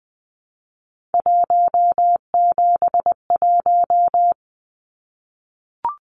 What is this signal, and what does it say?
Morse code, heard as a keyed tone, about 20 wpm, 700 Hz. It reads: 171
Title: Numbers - 3-Digit 20wpm